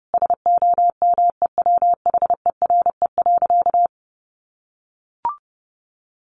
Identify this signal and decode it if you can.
Morse code, heard as a keyed tone, about 30 wpm, 700 Hz. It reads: SOMEWHERE.
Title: Sentences from Top 1400 Words 30wpm